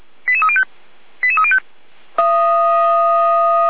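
Five-tone selective call (ZVEI), heard as quick runs of five tones, tones of about 70 ms. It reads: 88305 88305
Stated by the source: ZVEI five-tone selective call received on SDR.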